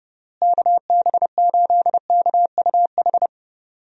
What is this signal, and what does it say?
Morse code, heard as a keyed tone, about 30 wpm, 700 Hz. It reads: KB8KUH
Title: USA Call Signs - 6 Characters - Repeated 2x - 30wpm